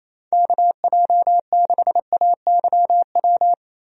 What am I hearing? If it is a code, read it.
Morse code, heard as a keyed tone, about 28 wpm, 700 Hz. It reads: KJ6AYW
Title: USA Call Signs - 6 Characters - Repeated 3x - 28wpm